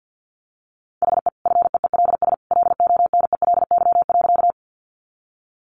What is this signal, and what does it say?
Morse code, heard as a keyed tone, about 48 wpm, 700 Hz. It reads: HE FEELS LONELY.
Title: Easy Phrases 50wpm